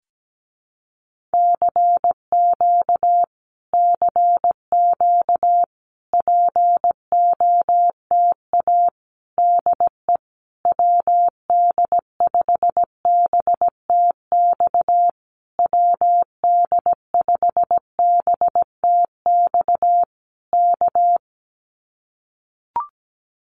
Morse code, heard as a keyed tone, about 17 wpm, 700 Hz. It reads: CQ CQ POTA DE WD5BTX WD5BTX K
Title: Common QSO Phrases Encore 17wpm